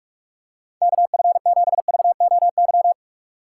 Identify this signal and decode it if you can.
Morse code, heard as a keyed tone, about 46 wpm, 700 Hz. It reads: KF7VOY
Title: USA Call Signs - 2x Character Spacing - 45wpm